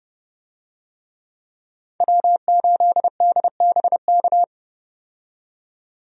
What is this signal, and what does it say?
Morse code, heard as a keyed tone, about 30 wpm, 700 Hz. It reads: W8DBK
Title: USA Call Signs - 5 Characters - 30wpm